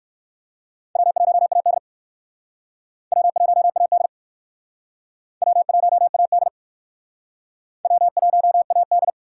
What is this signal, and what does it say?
Morse code, heard as a keyed tone, about 59 wpm, 700 Hz. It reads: W1AD W1AD W1AD W1AD